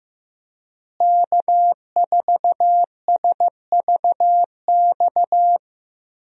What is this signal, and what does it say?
Morse code, heard as a keyed tone, about 15 wpm, 700 Hz. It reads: K4SVX
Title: USA Call Signs - 5 Characters - Repeated 3x - 15wpm